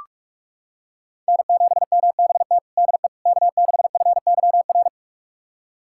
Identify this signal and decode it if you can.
Morse code, heard as a keyed tone, about 46 wpm, 700 Hz. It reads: N7MBT DE K6FYR
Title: USA Callsign DE Callsign - 5 Characters - 45wpm